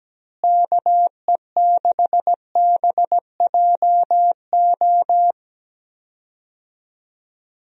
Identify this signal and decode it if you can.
Morse code, heard as a keyed tone, about 17 wpm, 700 Hz. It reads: KE6BJO
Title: USA Call Signs - 6 Characters - Repeated 3x - 17wpm